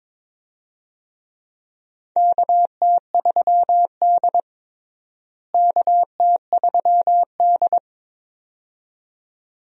Morse code, heard as a keyed tone, about 22 wpm, 700 Hz. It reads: KT3D KT3D
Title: USA Call Signs - 4 Characters - Repeated 2x - 22wpm